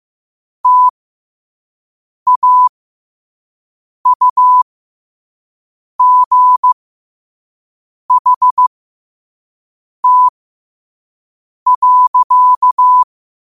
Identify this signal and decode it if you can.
Morse code, heard as a keyed tone, about 15 wpm, spaced as for 5 wpm, 1000 Hz. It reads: TAUGHT.
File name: qotdfiles / 5wpm